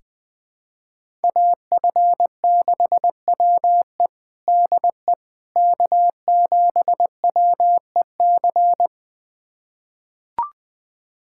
Morse code, heard as a keyed tone, about 21 wpm, 700 Hz. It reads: AF6WE DE K7WEC
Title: USA Callsign DE Callsign - 5 Characters - 20wpm